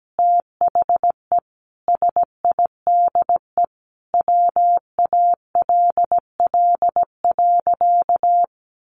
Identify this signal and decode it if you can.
Morse code, heard as a keyed tone, about 17 wpm, 700 Hz. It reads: THE SIDE WALL.
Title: Sentences from Top 1100 Words 17wpm